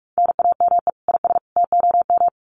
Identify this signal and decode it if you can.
Morse code, heard as a keyed tone, about 44 wpm, 700 Hz. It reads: NAME IS TOM
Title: Common QSO Phrases 45wpm